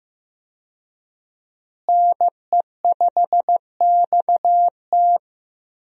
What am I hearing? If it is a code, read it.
Morse code, heard as a keyed tone, about 15 wpm, 700 Hz. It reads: NE5XT
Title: USA Call Signs - 5 Characters - Repeated 3x - 15wpm